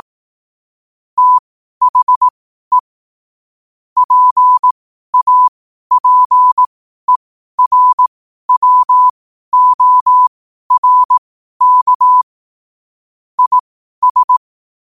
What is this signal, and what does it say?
Morse code, heard as a keyed tone, about 18 wpm, spaced as for 12 wpm, 1000 Hz. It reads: THE PAPERWORK IS
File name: qotdfiles / 12wpm